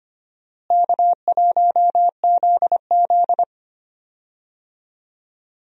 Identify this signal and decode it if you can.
Morse code, heard as a keyed tone, about 25 wpm, 700 Hz. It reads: K1ZZ
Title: USA Call Signs - 4 Characters - Repeated 3x - 25wpm